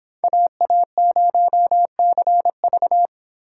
Morse code, heard as a keyed tone, about 24 wpm, 700 Hz. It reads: AA0CV